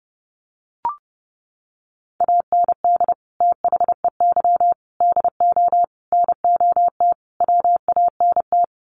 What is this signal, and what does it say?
Morse code, heard as a keyed tone, about 30 wpm, 700 Hz. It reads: AND THEY DO NOT WANT
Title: Sets of 5 Words Encore2 30wpm